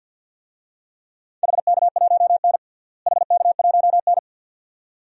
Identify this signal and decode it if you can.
Morse code, heard as a keyed tone, about 51 wpm, 700 Hz. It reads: HK1N HK1N
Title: DXCC-100 Call Signs - 4 Characters - Repeated 2x - 50wpm